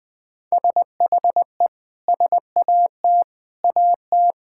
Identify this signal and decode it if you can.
Morse code, heard as a keyed tone, about 21 wpm, 700 Hz. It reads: SHE SAT AT
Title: Sentences with 6-Word Phrases - 20wpm